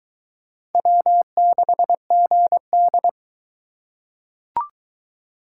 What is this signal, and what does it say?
Morse code, heard as a keyed tone, about 21 wpm, 700 Hz. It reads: W6GD